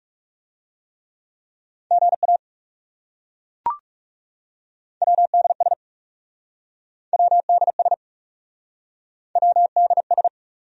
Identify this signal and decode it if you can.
Morse code, heard as a keyed tone, about 40 wpm, 700 Hz. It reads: GA WDS WDS WDS